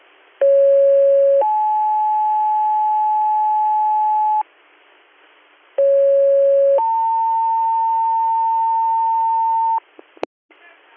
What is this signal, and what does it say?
Two-tone page, heard as a one-second tone, then a three-second tone, about 550 then 860 Hz, about 550 then 900 Hz, with speech. It is heard from 0.4 to 9.8 s.